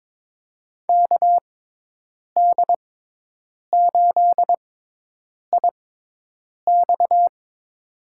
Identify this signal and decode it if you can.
Morse code, heard as a keyed tone, about 22 wpm, spaced as for 8 wpm, 700 Hz. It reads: KD8IX
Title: USA Call Signs - 5 Characters - 6x Character Spacing - 22wpm